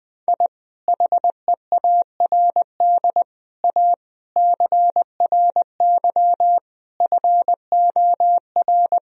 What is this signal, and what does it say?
Morse code, heard as a keyed tone, about 20 wpm, 700 Hz. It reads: I HEARD A CRY FOR
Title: Sentences from Top 1000 Words - Review Part 2 - 20wpm